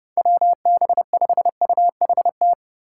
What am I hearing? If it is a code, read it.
Morse code, heard as a keyed tone, about 30 wpm, 700 Hz. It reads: WB5UHT
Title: USA Call Signs - 6 Characters - 3x Character Spacing - 30wpm